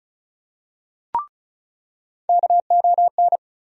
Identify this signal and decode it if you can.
Morse code, heard as a keyed tone, about 34 wpm, 700 Hz. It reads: KON
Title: USA Call Sign Postfixes - 3 Letters 35wpm